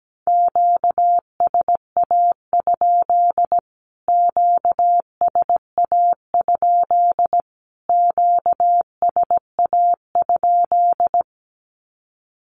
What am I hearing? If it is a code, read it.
Morse code, heard as a keyed tone, about 17 wpm, 700 Hz. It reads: QSA? QSA? QSA?
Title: Q-Codes Repeated 3x 17wpm